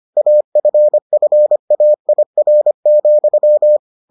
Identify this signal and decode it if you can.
Morse code, heard as a keyed tone, about 25 wpm, 600 Hz. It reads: AFFAIR,